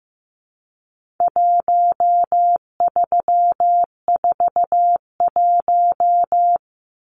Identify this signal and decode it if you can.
Morse code, heard as a keyed tone, about 15 wpm, 700 Hz. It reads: 1341
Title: Numbers - 4-Digit 15wpm